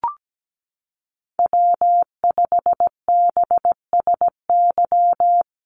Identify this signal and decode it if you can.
Morse code, heard as a keyed tone, about 17 wpm, 700 Hz. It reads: W5BSY
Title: USA Call Signs Repeated 2x 17wpm